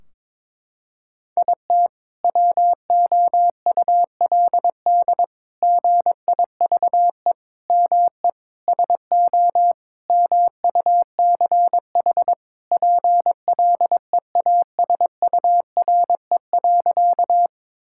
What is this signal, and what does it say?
Morse code, heard as a keyed tone, about 22 wpm, 700 Hz. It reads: IT WOULD GIVE ME SO MUCH PLEASURE.
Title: Sentences from Top 1600 Words 22wpm